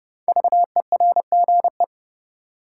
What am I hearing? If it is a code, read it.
Morse code, heard as a keyed tone, about 31 wpm, 700 Hz. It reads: VERGE
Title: Sets of 4 Words - New Word Drill - 30wpm